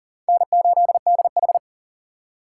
Morse code, heard as a keyed tone, about 40 wpm, 700 Hz. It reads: N8DH
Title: USA Call Signs - 4 Characters - 3x Character Spacing - 40wpm